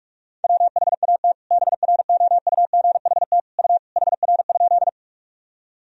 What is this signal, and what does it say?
Morse code, heard as a keyed tone, about 46 wpm, 690 Hz. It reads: WHAT BROUGHT U HR?
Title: Sentences from Top 500 Words and Abbreviations 45wpm